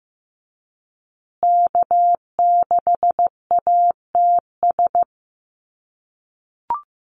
Morse code, heard as a keyed tone, about 15 wpm, 700 Hz. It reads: K6ATS
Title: USA Call Signs Repeated 2x 15wpm